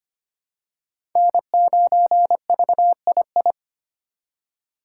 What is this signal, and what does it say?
Morse code, heard as a keyed tone, about 25 wpm, 700 Hz. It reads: N9VII
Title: USA Call Signs - 2x Character Spacing - 25wpm